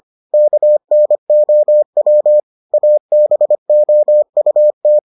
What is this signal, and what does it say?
Morse code, heard as a keyed tone, about 25 wpm, 600 Hz. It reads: KNOW ABOUT